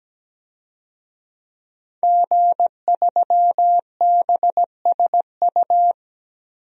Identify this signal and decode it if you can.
Morse code, heard as a keyed tone, about 17 wpm, 700 Hz. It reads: G3BSU
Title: DXCC-100 Call Signs - 5 Characters - Repeated 3x - 17wpm